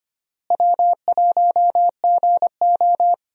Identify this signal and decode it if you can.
Morse code, heard as a keyed tone, about 25 wpm, 700 Hz. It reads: W1GO
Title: USA Call Signs - 4 Characters - Repeated 3x - 25wpm